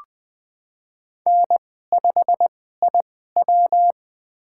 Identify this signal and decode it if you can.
Morse code, heard as a keyed tone, about 20 wpm, spaced as for 15 wpm, 700 Hz. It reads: N5IW
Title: USA Call Signs - 4 Characters - 2x Character Spacing - 20wpm